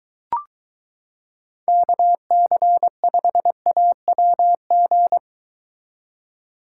Morse code, heard as a keyed tone, about 22 wpm, 700 Hz. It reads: KC5AWG